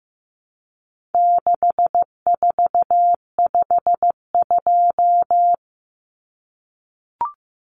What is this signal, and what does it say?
Morse code, heard as a keyed tone, about 15 wpm, 700 Hz. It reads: 6452
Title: Common Cut Numbers - 4-Digit 15wpm